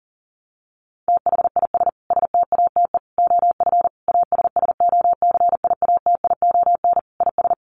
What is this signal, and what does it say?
Morse code, heard as a keyed tone, about 40 wpm, 700 Hz. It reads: THIS STATE OF ASSOCIATION IS
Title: Sentences from Top 1100 Words 40wpm